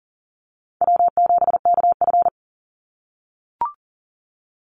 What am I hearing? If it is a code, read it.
Morse code, heard as a keyed tone, about 41 wpm, 700 Hz. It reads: W7KF
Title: Call Signs - 40wpm